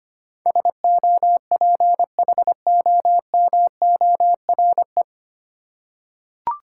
Morse code, heard as a keyed tone, about 25 wpm, 690 Hz. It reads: SOPHOMORE
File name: practice / 25wpm / 9 Letter Words - First 3 Letters Given - 25wpm